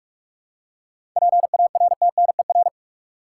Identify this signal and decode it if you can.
Morse code, heard as a keyed tone, about 46 wpm, 700 Hz. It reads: PARTNER
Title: Seven Letter Words - 45wpm